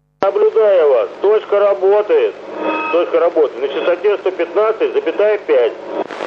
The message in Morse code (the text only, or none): none